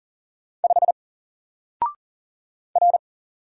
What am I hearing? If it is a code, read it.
Morse code, heard as a keyed tone, about 40 wpm, 700 Hz. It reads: 5 R
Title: Single Letter-Number Alt 40wpm